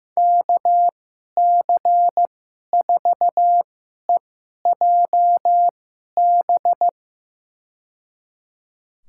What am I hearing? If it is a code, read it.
Morse code, heard as a keyed tone, about 15 wpm, spaced as for 11 wpm, 700 Hz. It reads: KC4EJB
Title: USA Call Signs - 2x Character Spacing - 15wpm